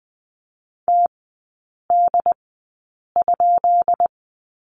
Morse code, heard as a keyed tone, about 20 wpm, spaced as for 8 wpm, 700 Hz. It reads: TD?
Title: 191 Lesson 31e - Letter v - Three Characters 20wpm